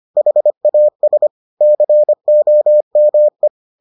Morse code, heard as a keyed tone, about 25 wpm, 600 Hz. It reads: HAS COME